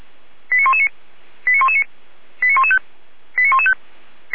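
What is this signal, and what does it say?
Five-tone selective call (ZVEI), heard as quick runs of five tones, tones of about 70 ms. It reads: 89119 89119 89115 89115
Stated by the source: ZVEI five-tone call 89123 received on SDR, Feuerwehr Orlamünde.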